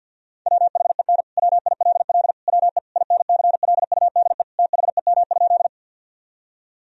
Morse code, heard as a keyed tone, about 52 wpm, 700 Hz. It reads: WHEN WILL WE INCLUDE THEM?
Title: Sentences from Top 1000 Words - Review Part 3 - 50wpm